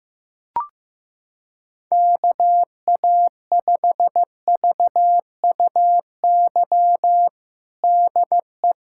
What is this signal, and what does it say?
Morse code, heard as a keyed tone, about 15 wpm, 700 Hz. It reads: KA5VUY DE
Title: USA Callsign DE Callsign - 6 Characters - 15wpm